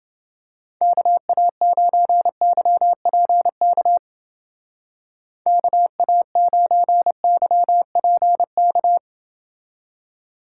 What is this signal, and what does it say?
Morse code, heard as a keyed tone, about 29 wpm, 700 Hz. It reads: KA9YPK KA9YPK